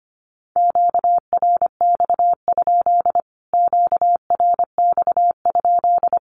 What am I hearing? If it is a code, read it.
Morse code, heard as a keyed tone, about 25 wpm, 700 Hz. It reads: QRX? QRX?
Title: Q-Codes Repeated 3x 25wpm